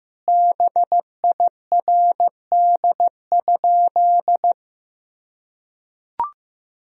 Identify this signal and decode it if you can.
Morse code, heard as a keyed tone, about 15 wpm, 700 Hz. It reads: BIRD?
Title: Sentences from Top 1000 Words - Review Part 2 - 15wpm